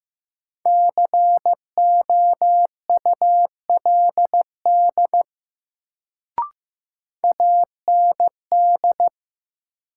Morse code, heard as a keyed tone, about 15 wpm, 700 Hz. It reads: COULD AND
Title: Sentences from Top 100 Words - New Word Drill - 15wpm